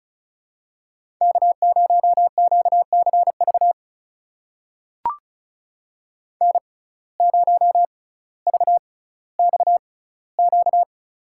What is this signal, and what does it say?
Morse code, heard as a keyed tone, about 36 wpm, spaced as for 18 wpm, 700 Hz. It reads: K0QCV N0VXQ
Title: USA Call Signs - 5 Characters - 6x Character Spacing - 35wpm